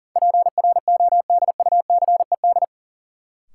Morse code, heard as a keyed tone, about 38 wpm, 700 Hz. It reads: PRODUCED
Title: Sentences from Top 1000 Words - Review Part 6 - New Word Drill - 40wpm